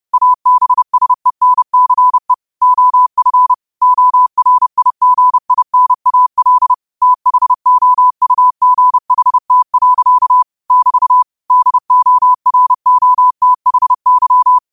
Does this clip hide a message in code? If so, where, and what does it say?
Yes: Morse code from 0.1 to 14.6 s, heard as a keyed tone, about 30 wpm, 1000 Hz. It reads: ABSENCE OF ORIGINAL THOUGHT. <BT> DOROTHY